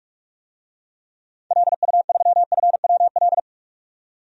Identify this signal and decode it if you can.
Morse code, heard as a keyed tone, about 46 wpm, 700 Hz. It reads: LU3FWL
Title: DXCC-100 Call Signs - 6 Characters - 45wpm